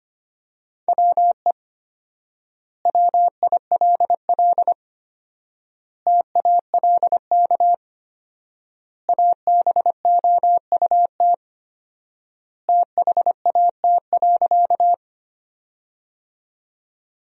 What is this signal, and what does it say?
Morse code, heard as a keyed tone, about 25 wpm, 700 Hz. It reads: WE WILL TALK ABOUT THAT.
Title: 4x Word-Spacing - Sentences from Top 400 Words - 25wpm